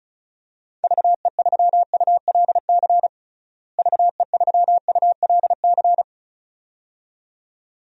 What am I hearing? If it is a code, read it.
Morse code, heard as a keyed tone, about 36 wpm, 700 Hz. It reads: VE3ULC VE3ULC
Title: DXCC-100 Call Signs - 6 Characters - Repeated 2x - 35wpm